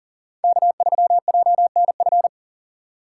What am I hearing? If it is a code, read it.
Morse code, heard as a keyed tone, about 40 wpm, 700 Hz. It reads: K3JNF
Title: USA Call Signs - 4x Character Spacing - 40wpm